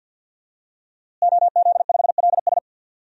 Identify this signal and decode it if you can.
Morse code, heard as a keyed tone, about 46 wpm, 700 Hz. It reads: OZ5LS
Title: DXCC-100 Call Signs - 5 Characters - Repeated 3x - 50wpm